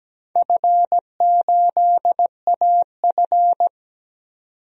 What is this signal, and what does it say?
Morse code, heard as a keyed tone, about 17 wpm, 700 Hz. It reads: F8AF
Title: DXCC-100 Call Signs - 4 Characters - 17wpm